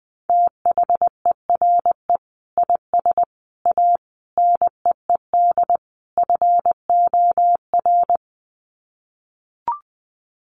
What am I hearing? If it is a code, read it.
Morse code, heard as a keyed tone, about 20 wpm, 700 Hz. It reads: THERE IS A NEED FOR
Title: Sets of 5 Words Encore2 20wpm